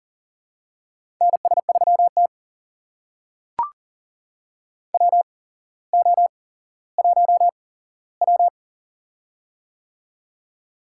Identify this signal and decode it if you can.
Morse code, heard as a keyed tone, about 40 wpm, spaced as for 17 wpm, 700 Hz. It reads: NS3T WO1W
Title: USA Call Signs - 4 Characters - 8x Character Spacing - 40wpm